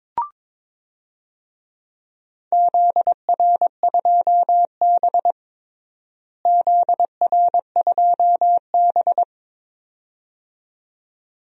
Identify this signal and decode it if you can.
Morse code, heard as a keyed tone, about 22 wpm, 700 Hz. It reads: ZR2B ZR2B
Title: DXCC-100 Call Signs - 4 Characters - Repeated 2x - 22wpm